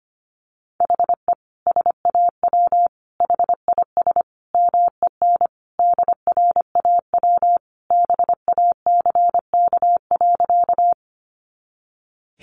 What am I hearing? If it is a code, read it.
Morse code, heard as a keyed tone, about 25 wpm, 700 Hz. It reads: HE SAW HIS MEN DRAW BACK.
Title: Sentences from Top 300 Words 25wpm